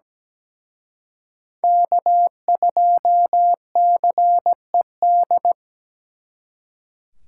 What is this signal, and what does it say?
Morse code, heard as a keyed tone, about 20 wpm, 700 Hz. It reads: K2CED
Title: Speed Racing - USA Call Signs 17wpm